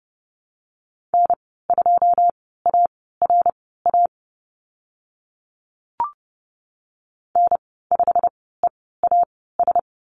Morse code, heard as a keyed tone, about 30 wpm, spaced as for 17 wpm, 700 Hz. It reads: N2ARA N5EAS
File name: Lesson 15g - Number 5 - Call Signs 30wpm